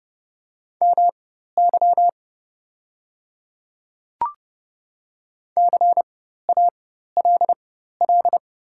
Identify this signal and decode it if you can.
Morse code, heard as a keyed tone, about 30 wpm, spaced as for 14 wpm, 690 Hz. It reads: MY CALL
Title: 4x Character Spacing - Top 100 Words - 30wpm